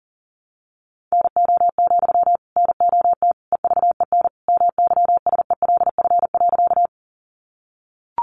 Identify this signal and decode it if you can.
Morse code, heard as a keyed tone, about 40 wpm, 700 Hz. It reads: NO, NOT EVEN MYSELF.
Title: Sentences from Top 1300 Words 40wpm